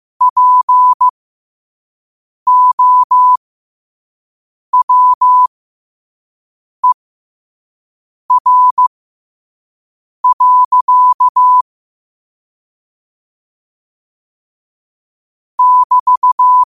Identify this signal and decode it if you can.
Morse code, heard as a keyed tone, about 15 wpm, spaced as for 5 wpm, 1000 Hz. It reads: POWER. <BT>
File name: qotdfiles / 5wpm